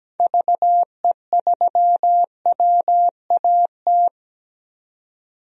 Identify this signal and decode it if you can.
Morse code, heard as a keyed tone, about 17 wpm, 700 Hz. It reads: VE3WAT
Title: DXCC-100 Call Signs - 6 Characters - 17wpm